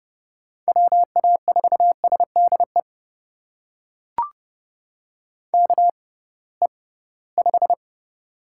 Morse code, heard as a keyed tone, about 30 wpm, spaced as for 19 wpm, 700 Hz. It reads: WA4SDE KE5
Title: USA Call Signs - 6 Characters - 6x Character Spacing - 30wpm